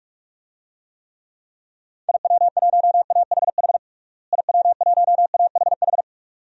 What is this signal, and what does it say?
Morse code, heard as a keyed tone, about 46 wpm, 700 Hz. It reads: IW1AHH IW1AHH
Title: DXCC-100 Call Signs - 6 Characters - Repeated 2x - 45wpm